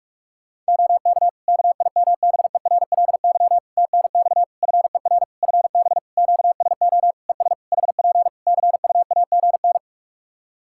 Morse code, heard as a keyed tone, about 46 wpm, 700 Hz. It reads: OK KIMBERLY TNX FER FB QSO ES HP CUAGN
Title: Practice QSO - Exchange 4 - Ending - 45wpm